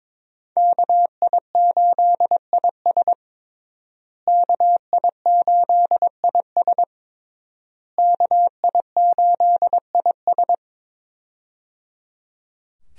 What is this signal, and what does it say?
Morse code, heard as a keyed tone, about 21 wpm, 700 Hz. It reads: KI8IS KI8IS KI8IS
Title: USA Call Signs - 5 Characters - Repeated 3x - 22wpm